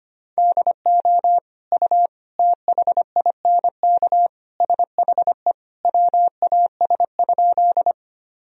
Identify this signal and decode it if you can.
Morse code, heard as a keyed tone, about 25 wpm, 700 Hz. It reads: DO U THINK SHE WAS?
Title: Sentences from Top 200 Words and Abbreviations 25wpm